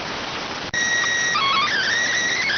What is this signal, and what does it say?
SSTV (slow-scan television), PD 290.